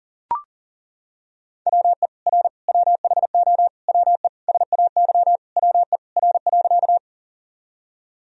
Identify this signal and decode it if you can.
Morse code, heard as a keyed tone, about 40 wpm, 700 Hz. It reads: WE R WHO WE SAY WE R.
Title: Sentences from Top 200 Words and Abbreviations 40wpm